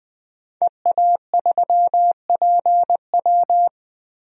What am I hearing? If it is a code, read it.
Morse code, heard as a keyed tone, about 20 wpm, 700 Hz. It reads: EA3PW
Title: DXCC-100 Call Signs - 5 Characters - 20wpm